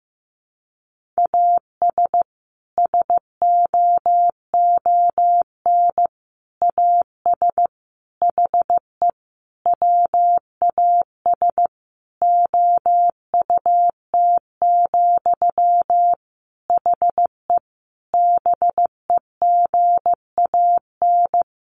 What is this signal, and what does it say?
Morse code, heard as a keyed tone, about 15 wpm, 700 Hz. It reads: AS SOON AS HE WAS OUT, HE BEGAN
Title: Sentences from Top 1500 Words 15wpm